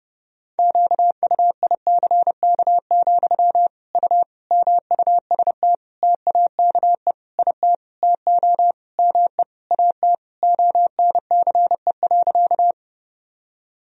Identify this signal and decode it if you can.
Morse code, heard as a keyed tone, about 30 wpm, 700 Hz. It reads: QUICK, U MUST TAKE IT TO ME AT ONCE.
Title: Sentences from Top 500 Words and Abbreviations 30wpm